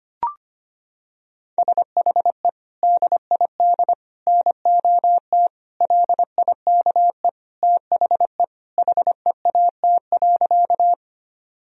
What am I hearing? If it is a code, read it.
Morse code, heard as a keyed tone, about 25 wpm, 700 Hz. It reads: SHE DID NOT LIKE THE HEAT.
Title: Sentences from Top 1000 Words - Review Part 3 - 25wpm